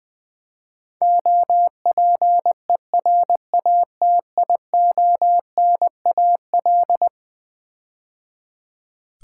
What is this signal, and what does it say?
Morse code, heard as a keyed tone, about 20 wpm, 700 Hz. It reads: OPERATIONAL
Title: Build A Word - 4 Letters At A Time - 11-Letter Words - 20wpm